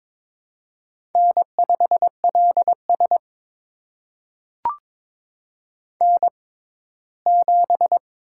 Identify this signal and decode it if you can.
Morse code, heard as a keyed tone, about 22 wpm, spaced as for 15 wpm, 700 Hz. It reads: N5LS N7
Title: USA Call Signs - 4 Characters - 6x Character Spacing - 22wpm